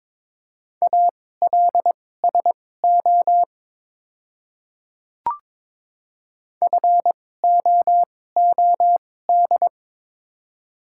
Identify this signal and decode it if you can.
Morse code, heard as a keyed tone, about 22 wpm, spaced as for 16 wpm, 700 Hz. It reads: ALSO FOOD
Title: 2x Character Spacing - Top 500 Words - 22wpm